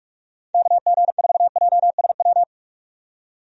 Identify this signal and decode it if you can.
Morse code, heard as a keyed tone, about 41 wpm, 700 Hz. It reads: KG4JSW
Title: USA Call Signs - 6 Characters - 45wpm